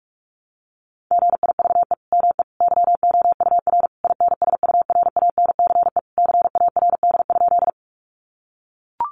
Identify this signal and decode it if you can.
Morse code, heard as a keyed tone, about 44 wpm, 700 Hz. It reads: GIVE ME YOUR INSURANCE CARD?
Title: Sentences from Top 1100 Words 45wpm